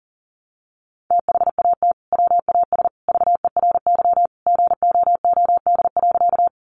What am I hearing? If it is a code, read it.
Morse code, heard as a keyed tone, about 40 wpm, 700 Hz. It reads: THAT WAS VERY GOOD.
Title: Sentences from Top 200 Words 40wpm